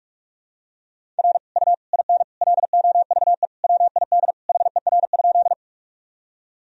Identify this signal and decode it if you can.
Morse code, heard as a keyed tone, about 41 wpm, 700 Hz. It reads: R U IN LOVE WID HER?
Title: Sentences from Top 400 Words and Abbreviations 45wpm